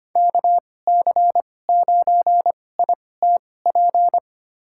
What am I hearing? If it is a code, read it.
Morse code, heard as a keyed tone, about 25 wpm, spaced as for 18 wpm, 700 Hz. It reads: KC9ITP
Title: USA Call Signs - 6 Characters - 2x Character Spacing - 25wpm